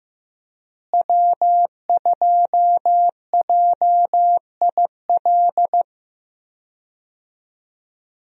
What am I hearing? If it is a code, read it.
Morse code, heard as a keyed tone, about 15 wpm, 700 Hz. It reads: W2JIL
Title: USA Call Signs - 5 Characters - Repeated 3x - 15wpm